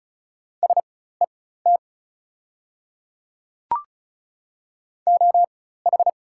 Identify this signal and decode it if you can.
Morse code, heard as a keyed tone, about 35 wpm, spaced as for 16 wpm, 690 Hz. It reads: SET OH
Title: 4x Character Spacing - Top 500 Words - 35wpm